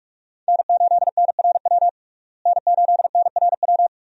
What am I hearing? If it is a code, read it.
Morse code, heard as a keyed tone, about 42 wpm, 700 Hz. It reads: N8NRW N8NRW
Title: USA Call Signs - 5 Characters - Repeated 2x - 45wpm